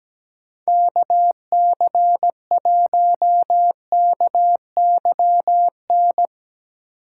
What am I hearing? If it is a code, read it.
Morse code, heard as a keyed tone, about 17 wpm, 700 Hz. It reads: KC1KYN
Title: Speed Racing - USA 2x3 Call Signs 17wpm